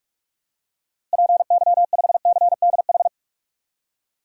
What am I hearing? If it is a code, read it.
Morse code, heard as a keyed tone, about 46 wpm, 700 Hz. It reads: PY5CDH
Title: DXCC-100 Call Signs - 6 Characters - Repeated 3x - 45wpm